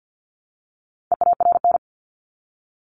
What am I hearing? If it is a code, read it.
Morse code, heard as a keyed tone, about 50 wpm, 700 Hz. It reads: EARN
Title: Top 1251-1500 Words 50wpm